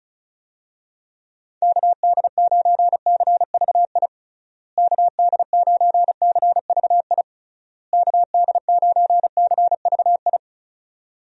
Morse code, heard as a keyed tone, about 36 wpm, 700 Hz. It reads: KD9CVI KD9CVI KD9CVI
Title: USA Call Signs - 6 Characters - Repeated 3x - 35wpm